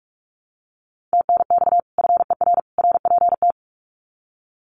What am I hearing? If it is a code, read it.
Morse code, heard as a keyed tone, about 44 wpm, 700 Hz. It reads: TNX FER RPT
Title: Common QSO Phrases 45wpm